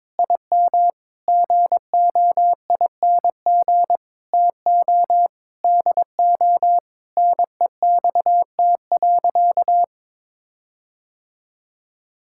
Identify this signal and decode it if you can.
Morse code, heard as a keyed tone, about 21 wpm, 700 Hz. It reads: IM GOING TO DO NEXT.
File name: practice / 22wpm / Sentences with 8-Word Phrases - 22wpm